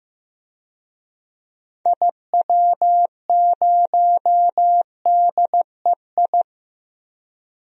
Morse code, heard as a keyed tone, about 15 wpm, 690 Hz. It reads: IW0DEI
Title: DXCC-100 Call Signs - 6 Characters - 15wpm